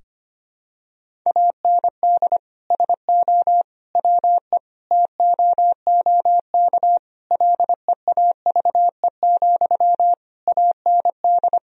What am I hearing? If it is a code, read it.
Morse code, heard as a keyed tone, about 25 wpm, 700 Hz. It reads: AND SO WE TOOK LEAVE, AND
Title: Sentences from Top 1000 Words - Review Part 1 - 25wpm